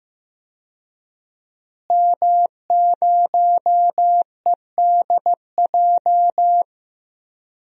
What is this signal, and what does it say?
Morse code, heard as a keyed tone, about 16 wpm, 700 Hz. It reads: M0EDJ
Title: DXCC-100 Call Signs - 5 Characters - Repeated 3x - 15wpm